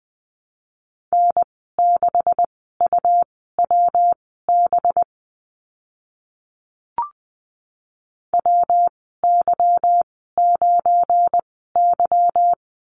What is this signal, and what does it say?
Morse code, heard as a keyed tone, about 20 wpm, spaced as for 14 wpm, 700 Hz. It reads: N6UWB WY9Y
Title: 193 Lesson 31g - Letter v - Call Signs 20wpm